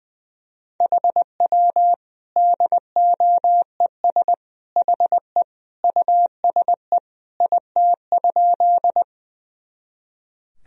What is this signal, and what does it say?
Morse code, heard as a keyed tone, about 20 wpm, 700 Hz. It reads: HW DOES HE USE IT?
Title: Sentences from Top 200 Words and Abbreviations 20wpm